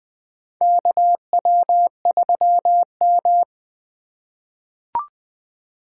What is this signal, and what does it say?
Morse code, heard as a keyed tone, about 21 wpm, 700 Hz. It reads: KW3M
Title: USA Call Signs - 4 Characters - 6x Character Spacing - 20wpm